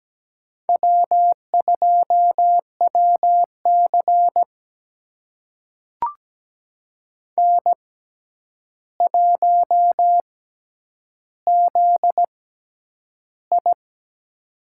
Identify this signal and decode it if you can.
Morse code, heard as a keyed tone, about 17 wpm, spaced as for 9 wpm, 700 Hz. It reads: W2WC N1ZI
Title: USA Call Signs - 6x Character Spacing - 17wpm